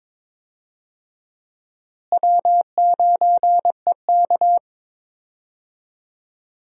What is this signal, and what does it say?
Morse code, heard as a keyed tone, about 22 wpm, 700 Hz. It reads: W9EK
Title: USA Call Signs - 4 Characters - Repeated 2x - 22wpm